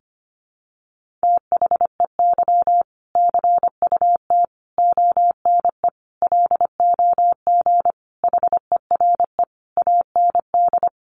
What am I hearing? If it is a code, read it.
Morse code, heard as a keyed tone, about 25 wpm, 700 Hz. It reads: THEY CUT ONE LOG HERE AND
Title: Sentences from Top 1400 Words 25wpm